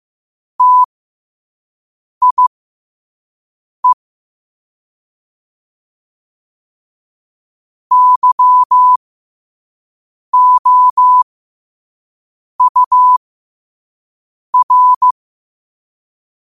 Morse code, heard as a keyed tone, about 15 wpm, spaced as for 5 wpm, 1000 Hz. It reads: TIE YOUR
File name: qotdfiles / 5wpm